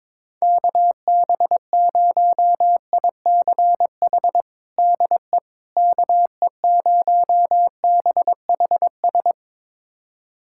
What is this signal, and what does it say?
Morse code, heard as a keyed tone, about 22 wpm, 700 Hz. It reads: KB0ICH DE KE0BHS
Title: USA Callsign DE Callsign - 6 Characters - 22wpm